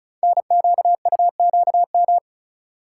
Morse code, heard as a keyed tone, about 35 wpm, 700 Hz. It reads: NQUQM
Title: Code Groups - 5 Letters - 35wpm